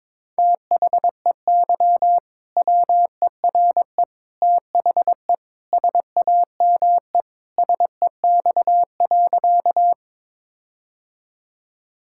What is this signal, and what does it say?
Morse code, heard as a keyed tone, about 21 wpm, 700 Hz. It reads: THEY WERE THE SAME SEX.
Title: Sentences from Top 1600 Words 22wpm